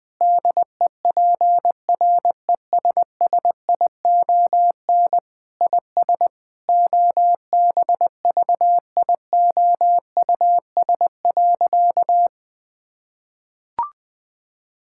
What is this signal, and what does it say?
Morse code, heard as a keyed tone, about 20 wpm, 700 Hz. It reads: DEPRESSION IS OBVIOUS.
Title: Sentences from Top 1600 Words 20wpm